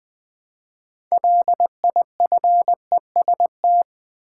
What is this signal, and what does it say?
Morse code, heard as a keyed tone, about 20 wpm, 700 Hz. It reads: LIFEST
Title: Build A Word - 3 Letters At A Time - 9-Letter Words - 20wpm